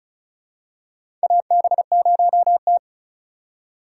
Morse code, heard as a keyed tone, about 34 wpm, 700 Hz. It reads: AB0T
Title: USA Call Signs - 4 Characters - 4x Character Spacing - 35wpm